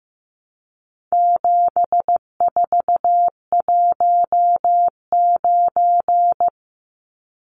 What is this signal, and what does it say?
Morse code, heard as a keyed tone, about 15 wpm, 700 Hz. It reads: 7419
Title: Numbers - 4-Digit 15wpm